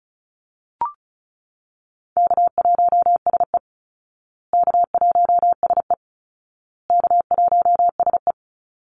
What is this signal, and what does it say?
Morse code, heard as a keyed tone, about 34 wpm, 700 Hz. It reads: K1SE K1SE K1SE
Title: Call Signs Repeated 2x - 35wpm